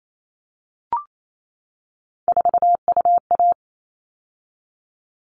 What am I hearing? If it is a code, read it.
Morse code, heard as a keyed tone, about 28 wpm, 700 Hz. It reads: 4UA
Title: All Cut Numbers - 3-Digit 28wpm